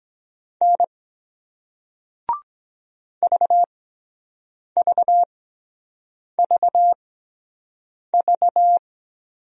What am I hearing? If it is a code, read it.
Morse code, heard as a keyed tone, about 23 wpm, spaced as for 7 wpm, 700 Hz. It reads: N VVVV